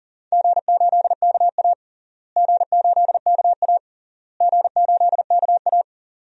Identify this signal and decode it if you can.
Morse code, heard as a keyed tone, about 40 wpm, 700 Hz. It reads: G8KA G8KA G8KA
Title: DXCC-100 Call Signs - 4 Characters - Repeated 3x - 40wpm